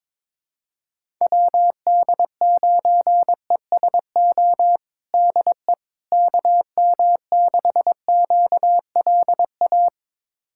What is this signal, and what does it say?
Morse code, heard as a keyed tone, about 22 wpm, 700 Hz. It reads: WD9ESO DE KM6QLA
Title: USA Callsign DE Callsign 22wpm